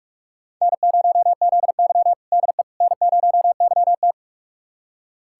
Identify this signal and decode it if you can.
Morse code, heard as a keyed tone, about 46 wpm, 700 Hz. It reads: N0ZY DE N0YT
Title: USA Callsign DE Callsign - 4 Characters - 45wpm